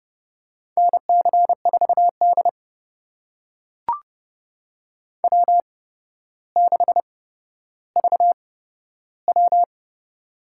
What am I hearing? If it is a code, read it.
Morse code, heard as a keyed tone, about 30 wpm, spaced as for 13 wpm, 700 Hz. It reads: NC4D W6VW
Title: USA Call Signs - 4 Characters - 8x Character Spacing - 30wpm